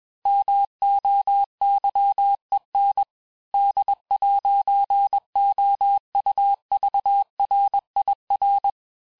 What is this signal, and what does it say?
Morse code, heard as a keyed tone, about 21 wpm, 800 Hz. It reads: MOYEN D'OUVRIR
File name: teste-FIN-de-QSO-vitesse-20mpm